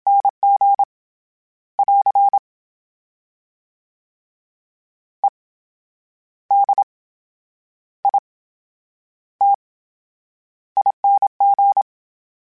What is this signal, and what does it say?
Morse code, heard as a keyed tone, about 27 wpm, spaced as for 10 wpm, 800 Hz. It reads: NG <AR> EDITING
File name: Special INGs / Special INGs-mp3s / VerySlowToFaster Ing @25